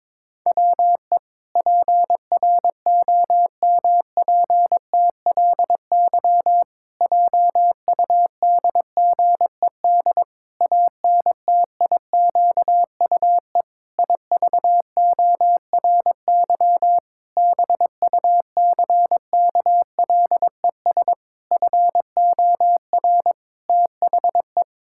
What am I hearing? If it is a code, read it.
Morse code, heard as a keyed tone, about 22 wpm, 700 Hz. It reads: WE PROMPTLY JUDGED ANTIQUE IVORY BUCKLES FOR THE
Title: Pangram Challenge v2 - 22wpm